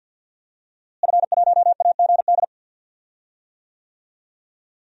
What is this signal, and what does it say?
Morse code, heard as a keyed tone, about 50 wpm, 700 Hz. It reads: F1AGD
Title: DXCC-100 Call Signs - 5 Characters - Repeated 3x - 50wpm